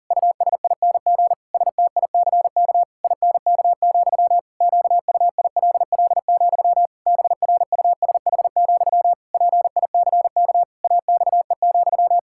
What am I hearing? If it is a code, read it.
Morse code, heard as a keyed tone, about 40 wpm, 700 Hz. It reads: USING STICK INK, QUILL, BRUSH, PICK AXE,
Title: Pangram Challenge v2 - 40wpm